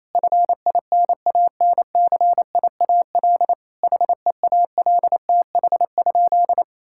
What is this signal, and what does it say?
Morse code, heard as a keyed tone, about 28 wpm, 700 Hz. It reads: FINANCIAL HEALTH?
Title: Sentences from Top 1000 Words - Review Part 5 - 28wpm